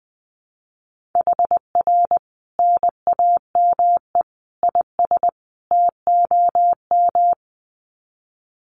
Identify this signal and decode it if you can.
Morse code, heard as a keyed tone, about 20 wpm, 700 Hz. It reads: HR NAME IS TOM
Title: Common QSO Phrases 20wpm